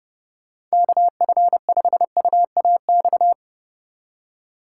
Morse code, heard as a keyed tone, about 36 wpm, 700 Hz. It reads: KF5UAX
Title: Speed Racing - USA 2x3 Call Signs 30wpm